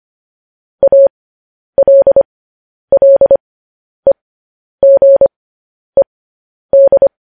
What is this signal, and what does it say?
Morse code, heard as a keyed tone, about 25 wpm, spaced as for 10 wpm, 550 Hz. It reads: ALLEGED
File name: podcast / News headlines-1x-25x10